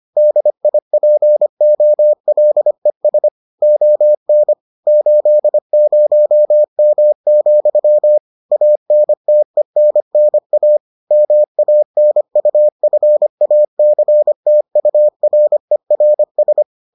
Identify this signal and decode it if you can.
Morse code, heard as a keyed tone, about 25 wpm, 600 Hz. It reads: DIPOLES ON 80M, ANTENNA MANUFACTURERS